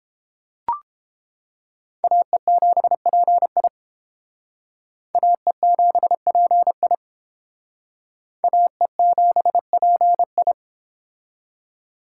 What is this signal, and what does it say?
Morse code, heard as a keyed tone, about 31 wpm, 700 Hz. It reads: AE7PI AE7PI AE7PI